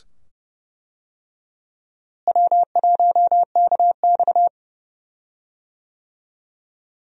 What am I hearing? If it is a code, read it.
Morse code, heard as a keyed tone, about 32 wpm, 700 Hz. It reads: W1KX
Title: USA Call Signs - 4 Characters - Repeated 2x - 30wpm